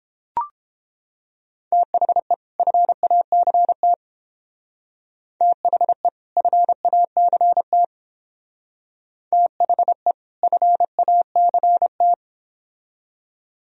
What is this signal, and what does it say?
Morse code, heard as a keyed tone, about 28 wpm, 700 Hz. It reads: THE FACT THE FACT THE FACT